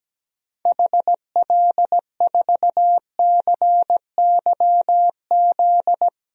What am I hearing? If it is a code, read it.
Morse code, heard as a keyed tone, about 17 wpm, 700 Hz. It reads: HL4CYZ
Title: DXCC-100 Call Signs - 6 Characters - 17wpm